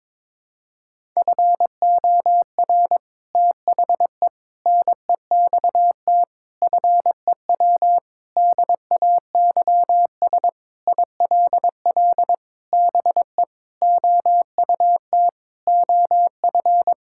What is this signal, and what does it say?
Morse code, heard as a keyed tone, about 22 wpm, 700 Hz. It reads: FOR THE NEXT FEW DAYS ILL BE OUT OF
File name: practice / 22wpm / Sentences with 3-Word Phrases - 22wpm